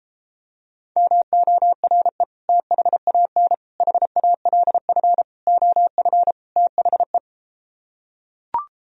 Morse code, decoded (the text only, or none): MORE THAN HALF OF THE